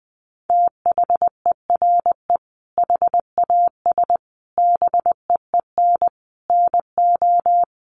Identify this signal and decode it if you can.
Morse code, heard as a keyed tone, about 20 wpm, 700 Hz. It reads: THERE HAS BEEN NO
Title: Sentences from Top 700 Words 20wpm